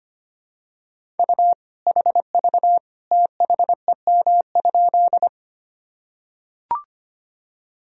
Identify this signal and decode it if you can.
Morse code, heard as a keyed tone, about 25 wpm, 700 Hz. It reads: U HV THEM?
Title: Sentences from Top 100 Words and Abbreviations 25wpm